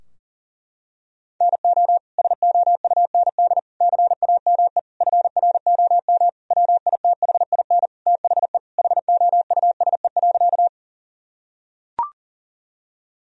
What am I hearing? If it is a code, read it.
Morse code, heard as a keyed tone, about 40 wpm, 700 Hz. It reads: NO SOUND CAME FROM WITHIN THE HOUSE.
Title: Sentences from Top 1000 Words - Review Part 4 - 40wpm